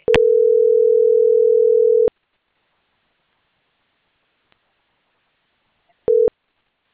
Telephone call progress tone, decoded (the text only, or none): ringback tone